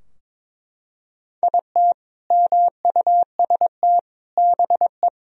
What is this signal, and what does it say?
Morse code, heard as a keyed tone, about 22 wpm, 700 Hz. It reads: IT MUST BE
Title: Sentences from Top 1000 Words - Review Part 2 - 22wpm